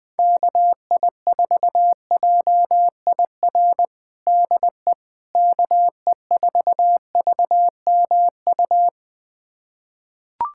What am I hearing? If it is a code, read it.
Morse code, heard as a keyed tone, about 20 wpm, 700 Hz. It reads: KI4JIR DE KE4VMU
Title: USA Callsign DE Callsign - 6 Characters - 20wpm